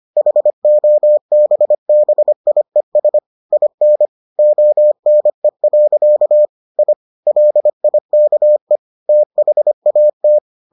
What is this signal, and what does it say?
Morse code, heard as a keyed tone, about 25 wpm, 600 Hz. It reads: HOBBIES IN ONE. I LIKE THAT